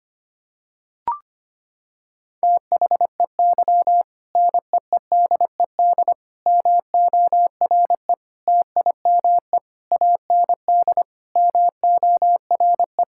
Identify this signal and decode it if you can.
Morse code, heard as a keyed tone, about 25 wpm, 700 Hz. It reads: THEY NEEDED MORE TIME AND MORE
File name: practice / 25wpm / Sentences with 2-Word Phrases - 25wpm